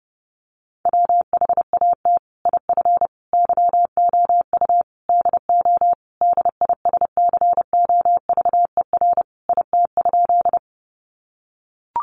Morse code, heard as a keyed tone, about 30 wpm, 700 Hz. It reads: WHAT IF YOU DO DISCOVER IT?
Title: Sentences from Top 1100 Words 30wpm